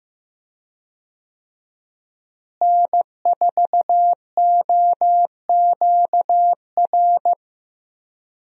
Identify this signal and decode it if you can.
Morse code, heard as a keyed tone, about 16 wpm, 700 Hz. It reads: N4OQR
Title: USA Call Signs - 5 Characters - Repeated 2x - 15wpm